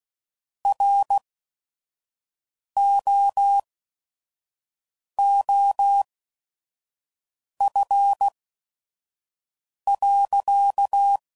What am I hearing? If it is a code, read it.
Morse code, heard as a keyed tone, about 16 wpm, spaced as for 5 wpm, 800 Hz. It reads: ROOF.
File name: CW Code Groups @5WPM / Weather Sets / Weather 4 24